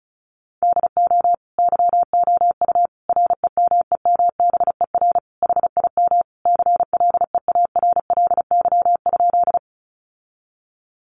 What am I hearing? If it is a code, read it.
Morse code, heard as a keyed tone, about 34 wpm, 700 Hz. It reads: DO YOU REMEMBER HIM CLEARLY?
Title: Sentences from Top 1300 Words 35wpm